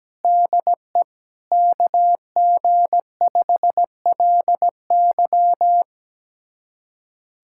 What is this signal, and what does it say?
Morse code, heard as a keyed tone, about 17 wpm, 700 Hz. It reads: DE KG5LY
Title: USA Callsign DE Callsign - 5 Characters - 17wpm